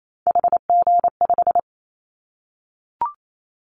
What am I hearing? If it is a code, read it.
Morse code, heard as a keyed tone, about 28 wpm, 700 Hz. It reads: HG5
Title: Callsign Prefixes for Top 25 DXCC - Repeated 3x - 28wpm